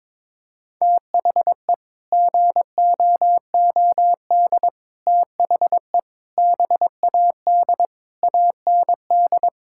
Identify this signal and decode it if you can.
Morse code, heard as a keyed tone, about 22 wpm, 700 Hz. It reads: THE GOOD THE BAD AND
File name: practice / 22wpm / Sets of 7 Words - 22wpm